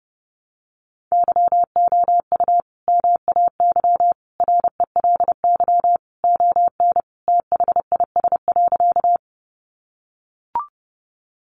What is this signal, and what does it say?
Morse code, heard as a keyed tone, about 30 wpm, 700 Hz. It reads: YOU MAY RELY ON THIS.
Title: Sentences from Top 1200 Words 30wpm